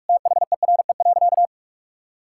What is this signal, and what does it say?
Morse code, heard as a keyed tone, about 46 wpm, 700 Hz. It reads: THERE.
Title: Sentences from Top 1000 Words - Review Part 6 - 45wpm